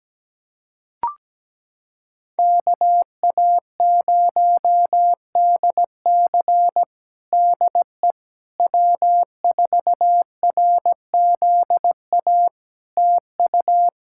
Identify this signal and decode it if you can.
Morse code, heard as a keyed tone, about 17 wpm, 700 Hz. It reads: KA0DC DE W4RZA TU
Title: Practice QSO - Exchange 4 - Ending - 17wpm